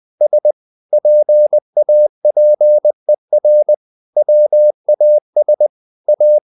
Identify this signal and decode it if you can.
Morse code, heard as a keyed tone, about 20 wpm, 600 Hz. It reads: S PAPER WAS A
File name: the old man and the sea / the old man and the sea part 1 20wpm